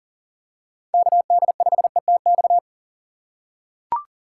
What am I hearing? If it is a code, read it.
Morse code, heard as a keyed tone, about 40 wpm, 700 Hz. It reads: KD5ETX
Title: USA Call Signs - 6 Characters - 6x Character Spacing - 40wpm